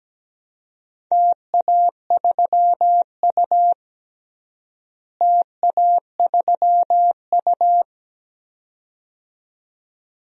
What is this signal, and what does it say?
Morse code, heard as a keyed tone, about 17 wpm, 700 Hz. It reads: TA3U TA3U
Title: DXCC-100 Call Signs - 4 Characters - Repeated 2x - 17wpm